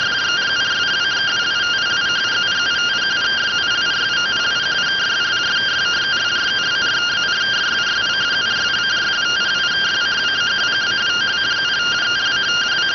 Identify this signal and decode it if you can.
Two-tone teleprinter signal, about 170 Hz shift.